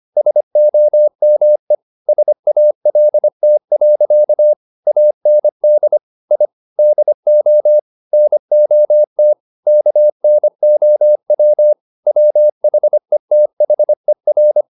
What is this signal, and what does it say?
Morse code, heard as a keyed tone, about 25 wpm, 600 Hz. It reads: SOME SALT. AND I DO NOT KNOW WHETHER
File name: the old man and the sea part 1 25wpm